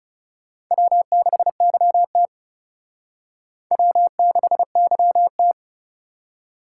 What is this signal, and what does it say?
Morse code, heard as a keyed tone, about 37 wpm, 700 Hz. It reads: W6YT W6YT
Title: Speed Racing - USA 1x2 Call Signs 30wpm